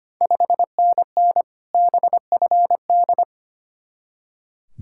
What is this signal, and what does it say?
Morse code, heard as a keyed tone, about 25 wpm, 690 Hz. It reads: 5NN BFD
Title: Ontario QSO Party Exchange 25wpm